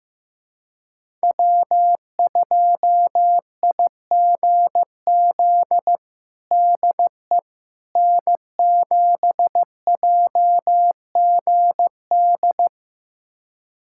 Morse code, heard as a keyed tone, about 15 wpm, 700 Hz. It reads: W2IGZ DE N7JGD
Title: USA Callsign DE Callsign - 5 Characters - 15wpm